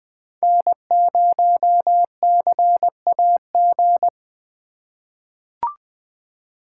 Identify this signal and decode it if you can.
Morse code, heard as a keyed tone, about 20 wpm, 700 Hz. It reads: N0CAG
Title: USA Call Signs - 5 Characters - 4x Character Spacing - 20wpm